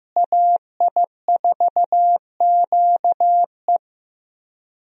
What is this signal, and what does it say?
Morse code, heard as a keyed tone, about 15 wpm, 700 Hz. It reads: AI4QE
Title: USA Call Signs - 5 Characters - Repeated 3x - 15wpm